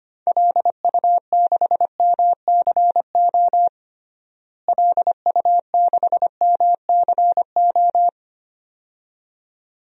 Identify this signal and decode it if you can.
Morse code, heard as a keyed tone, about 25 wpm, 700 Hz. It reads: LU6MCO LU6MCO
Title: DXCC-100 Call Signs - 6 Characters - Repeated 2x - 25wpm